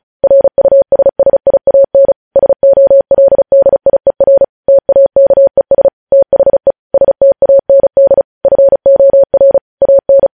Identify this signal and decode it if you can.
Morse code, heard as a keyed tone, about 36 wpm, 550 Hz. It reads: RUSSIAN SOLDIER TAKES THE STAND FOR AN
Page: news headlines in morse code at 35 wpm